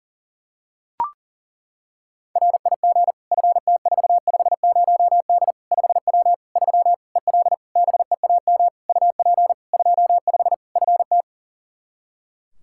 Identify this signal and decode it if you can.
Morse code, heard as a keyed tone, about 40 wpm, 700 Hz. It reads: RIG FT450D 5W 3 EL BEAM UP 25 FT
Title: Practice QSO - Exchange 2 - Info - 40wpm